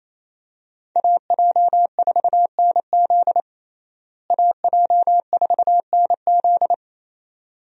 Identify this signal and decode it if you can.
Morse code, heard as a keyed tone, about 28 wpm, 700 Hz. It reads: AJ4NZ AJ4NZ
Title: USA Call Signs - 5 Characters - Repeated 2x - 28wpm